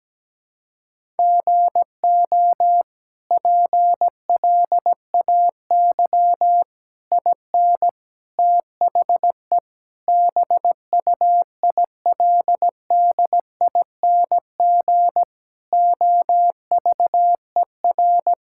Morse code, heard as a keyed tone, about 17 wpm, 700 Hz. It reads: GO PLAY IN THE BUILDING OVER
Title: Sentences from Top 1000 Words - Review Part 5 - 17wpm